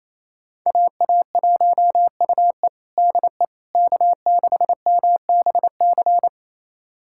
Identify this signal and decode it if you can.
Morse code, heard as a keyed tone, about 28 wpm, 700 Hz. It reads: AA1UE DE K6MBC
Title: USA Callsign DE Callsign - 5 Characters - 28wpm